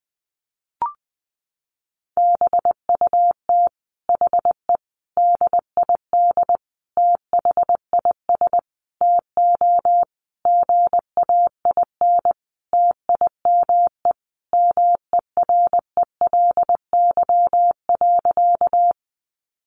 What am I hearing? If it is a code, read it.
Morse code, heard as a keyed tone, about 20 wpm, 700 Hz. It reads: BUT HE DID THIS TO GAIN TIME MERELY.
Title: Sentences from Top 1400 Words 20wpm